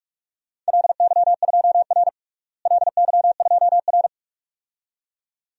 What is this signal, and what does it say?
Morse code, heard as a keyed tone, about 42 wpm, 700 Hz. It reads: LY2R LY2R
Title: DXCC-100 Call Signs - 4 Characters - Repeated 2x - 45wpm